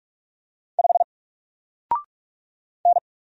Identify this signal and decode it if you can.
Morse code, heard as a keyed tone, about 46 wpm, 700 Hz. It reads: 5 N
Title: Single Letter-Number 45wpm